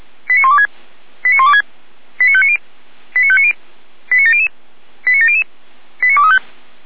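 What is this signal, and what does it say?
Five-tone selective call (ZVEI), heard as quick runs of five tones, tones of about 70 ms. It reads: 89126 89126 89590 89590 89700 89700 89235
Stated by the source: ZVEI five-tone selective call received on an SDR.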